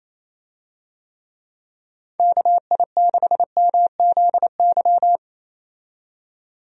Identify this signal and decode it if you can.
Morse code, heard as a keyed tone, about 28 wpm, 700 Hz. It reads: KI6MZY
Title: USA Call Signs - 6 Characters - 28wpm